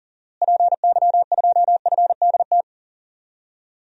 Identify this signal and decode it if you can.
Morse code, heard as a keyed tone, about 40 wpm, 700 Hz. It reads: PY2FDT
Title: DXCC-100 Call Signs - 6 Characters - Repeated 2x - 40wpm